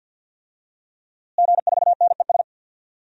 Morse code, heard as a keyed tone, about 50 wpm, 700 Hz. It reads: G4NES
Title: DXCC-100 Call Signs - 5 Characters - Repeated 2x - 50wpm